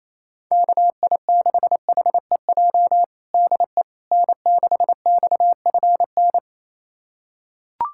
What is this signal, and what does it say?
Morse code, heard as a keyed tone, about 27 wpm, 700 Hz. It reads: KI6HEJ DE N6XFN
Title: USA Callsign DE Callsign 28wpm